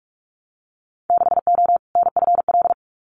Morse code, heard as a keyed tone, about 44 wpm, 700 Hz. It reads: 6O NFL
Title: Winter Field Day Exchanges 45wpm